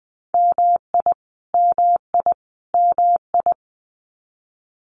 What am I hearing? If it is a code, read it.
Morse code, heard as a keyed tone, about 20 wpm, 700 Hz. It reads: MI MI MI
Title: US States Repeated 3x 20wpm